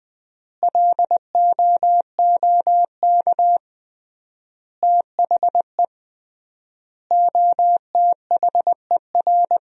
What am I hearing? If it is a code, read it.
Morse code, heard as a keyed tone, about 20 wpm, 700 Hz. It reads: LOOK THE OTHER
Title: 3x Word-Spacing - Sentences from Top 100 Words - 20wpm